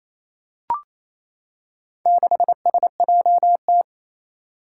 Morse code, heard as a keyed tone, about 28 wpm, 700 Hz. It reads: 6SJT
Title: Four Challenging Letter-Number Combinations 28wpm